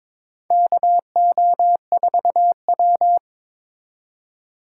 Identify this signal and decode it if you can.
Morse code, heard as a keyed tone, about 22 wpm, 700 Hz. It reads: KO4W
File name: practice / 22wpm / Four Letter-Number Combinations - 22wpm